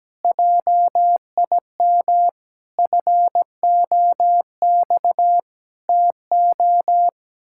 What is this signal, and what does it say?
Morse code, heard as a keyed tone, about 17 wpm, 700 Hz. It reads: JIM FOX TO
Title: Pangram Challenge v2 - 17wpm